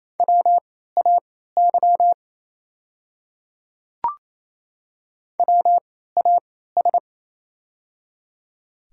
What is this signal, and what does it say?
Morse code, heard as a keyed tone, about 28 wpm, spaced as for 16 wpm, 700 Hz. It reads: WAY WAS
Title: 3x Character Spacing - Top 100 Words - 28wpm